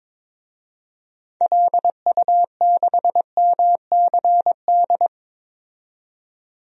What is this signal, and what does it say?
Morse code, heard as a keyed tone, about 22 wpm, 700 Hz. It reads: LU6MCD
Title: DXCC-100 Call Signs - 6 Characters - 22wpm